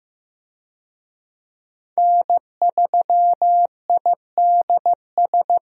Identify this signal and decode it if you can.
Morse code, heard as a keyed tone, about 16 wpm, 700 Hz. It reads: N3IDS
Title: USA Call Signs - 5 Characters - 15wpm